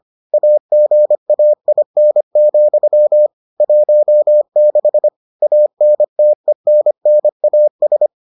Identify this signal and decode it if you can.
Morse code, heard as a keyed tone, about 25 wpm, 600 Hz. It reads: AGAIN, 16 ANTENNAS